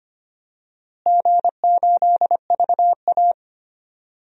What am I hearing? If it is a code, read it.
Morse code, heard as a keyed tone, about 25 wpm, 700 Hz. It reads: G8VA
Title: DXCC-100 Call Signs - 4 Characters - Repeated 2x - 25wpm